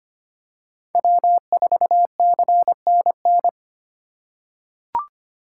Morse code, heard as a keyed tone, about 25 wpm, 700 Hz. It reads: W4CNN
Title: USA Call Signs - 4x Character Spacing - 25wpm